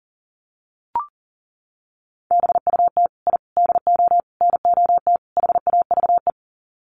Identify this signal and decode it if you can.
Morse code, heard as a keyed tone, about 40 wpm, 700 Hz. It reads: BUT I DO NOT HAVE
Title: Sets of 5 Words Encore 40wpm